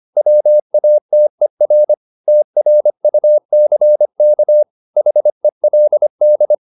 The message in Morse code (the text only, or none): WATER TRUCK HELD